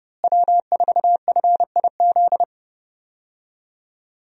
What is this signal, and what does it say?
Morse code, heard as a keyed tone, about 31 wpm, 700 Hz. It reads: W4FIZ
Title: USA Call Signs - 5 Characters - Repeated 2x - 30wpm